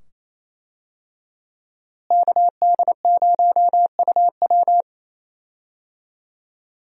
Morse code, heard as a keyed tone, about 28 wpm, 700 Hz. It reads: KD0UW
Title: USA Call Signs - 5 Characters - Repeated 2x - 28wpm